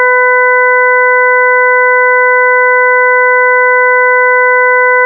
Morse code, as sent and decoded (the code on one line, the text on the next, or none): none